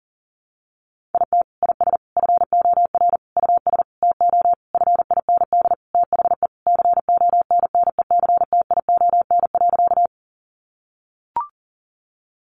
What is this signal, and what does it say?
Morse code, heard as a keyed tone, about 40 wpm, 700 Hz. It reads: IT IS FOR US TO FIND THE CONNECTION.
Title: Sentences from Top 1300 Words 40wpm